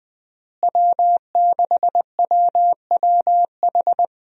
Morse code, heard as a keyed tone, about 21 wpm, 700 Hz. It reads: W6WWH
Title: USA Call Signs - 5 Characters - Repeated 3x - 20wpm